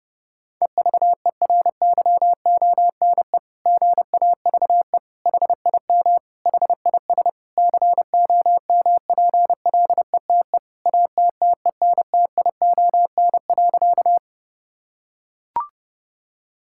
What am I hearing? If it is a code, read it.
Morse code, heard as a keyed tone, about 32 wpm, 700 Hz. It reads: EVERYONE GAVE HIM HIS COMPLETE ATTENTION.
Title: Sentences from Top 1000 Words - Review Part 6 - 30wpm